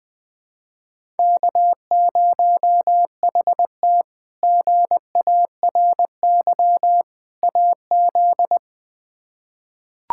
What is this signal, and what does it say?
Morse code, heard as a keyed tone, about 20 wpm, 700 Hz. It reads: K0HT GARY AZ
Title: K1USN SST QSO Practice - 20wpm